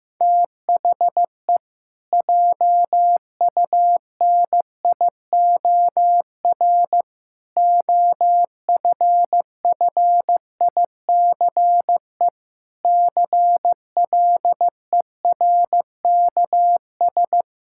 Morse code, heard as a keyed tone, about 15 wpm, 700 Hz. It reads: THE JUNIOR OFFICE CLERKS
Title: Pangram Challenge v2 - 15wpm